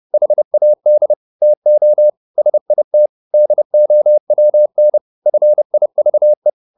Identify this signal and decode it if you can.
Morse code, heard as a keyed tone, about 30 wpm, 600 Hz. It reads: HAD TO SIT DOWN FIVE